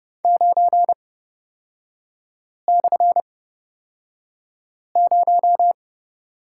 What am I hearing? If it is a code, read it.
Morse code, heard as a keyed tone, about 30 wpm, spaced as for 5 wpm, 700 Hz. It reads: 9/0
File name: Lesson 36e - BT - Three Characters 30wpm